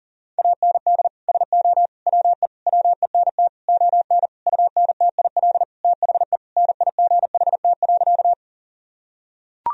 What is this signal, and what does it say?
Morse code, heard as a keyed tone, about 37 wpm, 700 Hz. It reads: AND SO WE WENT ON UNTIL THE NIGHT.
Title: Sentences from Top 1000 Words - Review Part 1 - 40wpm